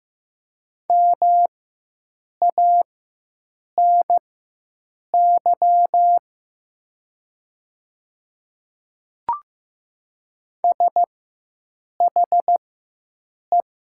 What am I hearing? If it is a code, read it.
Morse code, heard as a keyed tone, about 15 wpm, spaced as for 7 wpm, 700 Hz. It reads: MANY SHE
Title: 4x Character Spacing - Top 100 Words - 15wpm